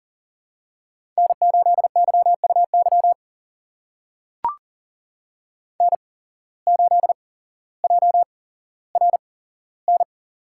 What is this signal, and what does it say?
Morse code, heard as a keyed tone, about 38 wpm, spaced as for 17 wpm, 700 Hz. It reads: N8YUY N8JRN
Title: USA Call Signs - 8x Character Spacing - 40wpm